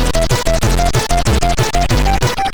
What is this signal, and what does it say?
There is no code.